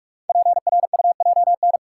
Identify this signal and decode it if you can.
Morse code, heard as a keyed tone, about 44 wpm, 700 Hz. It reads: PRUJN
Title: Code Groups - 5 Letters - 45wpm